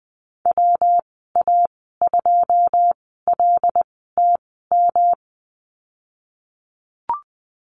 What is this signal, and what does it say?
Morse code, heard as a keyed tone, about 20 wpm, spaced as for 15 wpm, 700 Hz. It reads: WA2LTM
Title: 140 Lesson 23g - Letter f - Call Signs 20wpm